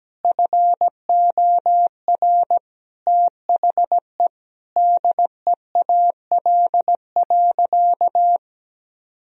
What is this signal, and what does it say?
Morse code, heard as a keyed tone, about 17 wpm, 700 Hz. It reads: FOR THE DEAL.
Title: Sentences from Top 1000 Words - Review Part 4 - 17wpm